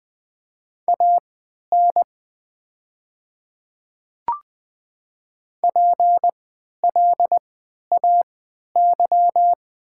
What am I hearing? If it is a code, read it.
Morse code, heard as a keyed tone, about 20 wpm, spaced as for 11 wpm, 700 Hz. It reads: AN PLAY
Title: 3x Character Spacing - Top 500 Words - 20wpm